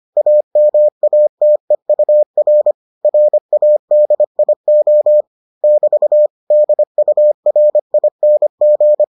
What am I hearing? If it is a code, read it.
Morse code, heard as a keyed tone, about 25 wpm, 600 Hz. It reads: AMATEUR RADIO <BT> DURING